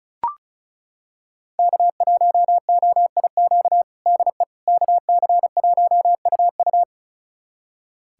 Morse code, heard as a keyed tone, about 34 wpm, 700 Hz. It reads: K1OIQ DE KC1UU
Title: USA Callsign DE Callsign - 5 Characters - 35wpm